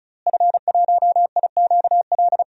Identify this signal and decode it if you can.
Morse code, heard as a keyed tone, about 36 wpm, 700 Hz. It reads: F1IQL
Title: DXCC-100 Call Signs - 5 Characters - Repeated 2x - 35wpm